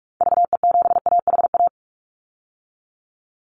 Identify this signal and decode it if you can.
Morse code, heard as a keyed tone, about 44 wpm, 700 Hz. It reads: VE7AHA
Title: Call Signs Repeated 2x - 45wpm